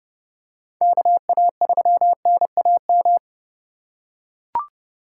Morse code, heard as a keyed tone, about 28 wpm, 700 Hz. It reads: KA3NAM
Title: K1USN SST QSO Practice - 30wpm